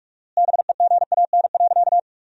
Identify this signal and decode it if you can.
Morse code, heard as a keyed tone, about 47 wpm, 700 Hz. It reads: BEGAN.
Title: 4x Word-Spacing - Sentences from Top 300 Words - 45wpm